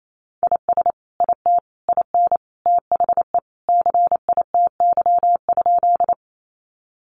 Morse code, heard as a keyed tone, about 28 wpm, 700 Hz. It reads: IS IT IN THE CITY?
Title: Sentences from Top 300 Words 28wpm